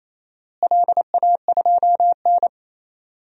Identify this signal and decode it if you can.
Morse code, heard as a keyed tone, about 28 wpm, 700 Hz. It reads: LA2N
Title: DXCC-100 Call Signs - 4 Characters - Repeated 2x - 28wpm